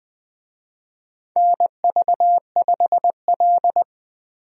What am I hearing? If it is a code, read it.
Morse code, heard as a keyed tone, about 20 wpm, 700 Hz. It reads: NV5L
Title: USA Call Signs - 4 Characters - Repeated 3x - 20wpm